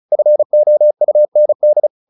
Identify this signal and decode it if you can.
Morse code, heard as a keyed tone, about 34 wpm, 600 Hz. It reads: FOUND